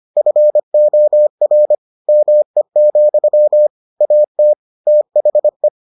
Morse code, heard as a keyed tone, about 25 wpm, 600 Hz. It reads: FOR ME, AT THE